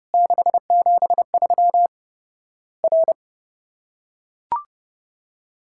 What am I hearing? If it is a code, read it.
Morse code, heard as a keyed tone, about 30 wpm, 700 Hz. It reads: 673 R
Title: Worked All Europe DX Contest - QTC Exchange - 30wpm